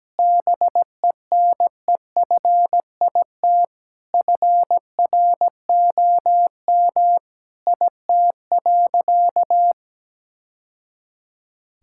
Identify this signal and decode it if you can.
Morse code, heard as a keyed tone, about 17 wpm, 700 Hz. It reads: BENEFIT FROM IT.
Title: Sentences from Top 1000 Words - Review Part 6 - 17wpm